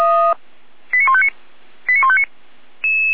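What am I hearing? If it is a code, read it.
Five-tone selective call (ZVEI), heard as quick runs of five tones, tones of about 70 ms. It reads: 89159 89159
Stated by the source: ZVEI five-tone selective call received on SDR, 455.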